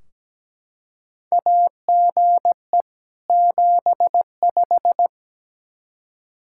Morse code, heard as a keyed tone, about 17 wpm, 700 Hz. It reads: AGE 75
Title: K1USN SST QSO Practice 17wpm